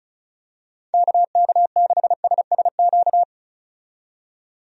Morse code, heard as a keyed tone, about 36 wpm, 700 Hz. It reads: KK6SSQ
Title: USA Call Signs - 6 Characters - 35wpm